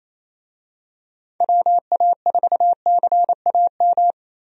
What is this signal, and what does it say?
Morse code, heard as a keyed tone, about 28 wpm, 700 Hz. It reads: WA4CAM
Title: USA Call Signs - 6 Characters - Repeated 2x - 28wpm